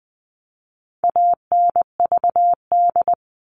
Morse code, heard as a keyed tone, about 20 wpm, 700 Hz. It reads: ANVD
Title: All Cut Numbers - 4-Digit 20wpm